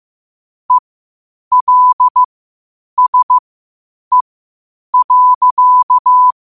Morse code, heard as a keyed tone, about 15 wpm, spaced as for 8 wpm, 1000 Hz. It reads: ELSE.
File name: qotdfiles / 8wpm